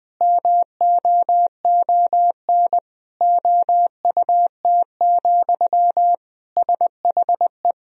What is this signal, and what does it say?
Morse code, heard as a keyed tone, about 20 wpm, 700 Hz. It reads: MOON OUT, SHE
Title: Sentences from Top 500 Words and Abbreviations 20wpm